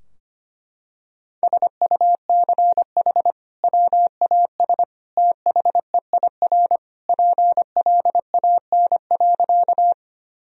Morse code, heard as a keyed tone, about 25 wpm, 700 Hz. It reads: SUCH WAS THEIR PLAN.
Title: Sentences from Top 1000 Words - Review Part 2 - 25wpm